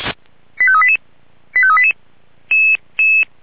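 Five-tone selective call (ZVEI), heard as quick runs of five tones, tones of about 70 ms. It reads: 85388 85388
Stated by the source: ZVEI five-tone selective call 85388, received via SDR.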